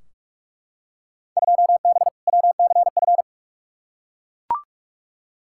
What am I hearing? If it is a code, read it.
Morse code, heard as a keyed tone, about 42 wpm, 700 Hz. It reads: 2B WCF
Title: Field Day Exchanges Encore - 45wpm